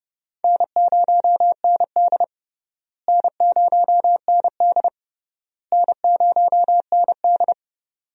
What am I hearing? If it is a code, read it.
Morse code, heard as a keyed tone, about 30 wpm, 700 Hz. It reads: N0ND N0ND N0ND
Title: USA Call Signs - 4 Characters - Repeated 3x - 30wpm